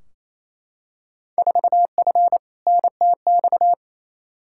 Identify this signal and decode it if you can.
Morse code, heard as a keyed tone, about 28 wpm, 700 Hz. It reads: 4F NTX
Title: Field Day Exchanges Encore - 28wpm